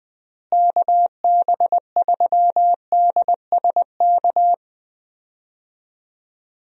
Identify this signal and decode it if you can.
Morse code, heard as a keyed tone, about 19 wpm, 700 Hz. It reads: KB3DSK
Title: USA Call Signs - 6 Characters - Repeated 2x - 20wpm